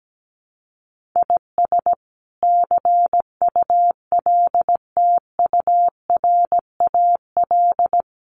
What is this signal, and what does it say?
Morse code, heard as a keyed tone, about 17 wpm, 700 Hz. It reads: IS CULTURAL
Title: Sentences from Top 1500 Words 17wpm